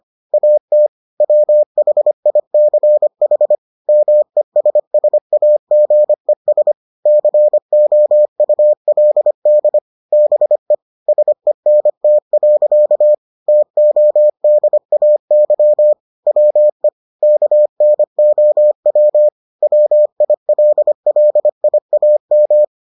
Morse code, heard as a keyed tone, about 25 wpm, 600 Hz. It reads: AT WHICH MESSAGES COULD BE SENT. TODAY WE KNOW WILLIAM